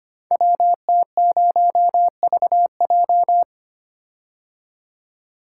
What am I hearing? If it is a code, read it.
Morse code, heard as a keyed tone, about 25 wpm, 700 Hz. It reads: WT0VJ
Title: Code Groups - 5 Letter:Numbers - 25wpm